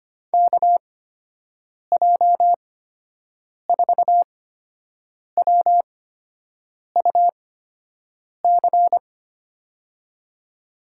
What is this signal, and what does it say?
Morse code, heard as a keyed tone, about 25 wpm, spaced as for 7 wpm, 700 Hz. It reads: KJ4WUC
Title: USA Call Signs - 8x Character Spacing - 25wpm